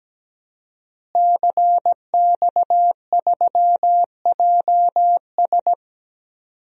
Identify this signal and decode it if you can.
Morse code, heard as a keyed tone, about 17 wpm, 700 Hz. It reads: CX3JS
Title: DXCC-100 Call Signs - 5 Characters - Repeated 2x - 17wpm